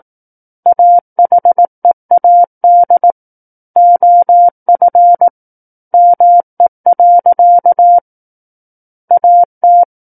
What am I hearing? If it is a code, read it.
Morse code, heard as a keyed tone, about 18 wpm, 700 Hz. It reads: AHEAD OF ME. AT
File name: 18 WPM